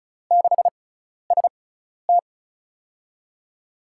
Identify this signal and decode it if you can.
Morse code, heard as a keyed tone, about 36 wpm, spaced as for 12 wpm, 700 Hz. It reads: 6ST
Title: USA Call Signs - 6x Character Spacing - 35wpm